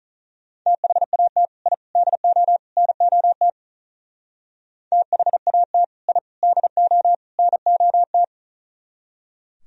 Morse code, heard as a keyed tone, about 44 wpm, 700 Hz. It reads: THAT I DO NOT THAT I DO NOT